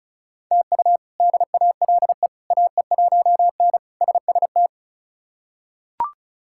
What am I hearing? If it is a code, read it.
Morse code, heard as a keyed tone, about 36 wpm, 700 Hz. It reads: TU DALE AE1N SST
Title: K1USN SST QSO Practice - 35wpm